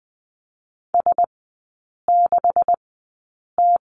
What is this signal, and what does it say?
Morse code, heard as a keyed tone, about 20 wpm, spaced as for 8 wpm, 700 Hz. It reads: S6T
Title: 127 Lesson 21e - Number 6 - Three Characters 20wpm